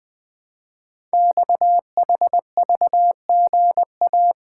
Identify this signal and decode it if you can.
Morse code, heard as a keyed tone, about 20 wpm, 700 Hz. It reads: XHVGA
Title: Code Groups - 5 Letters - 20wpm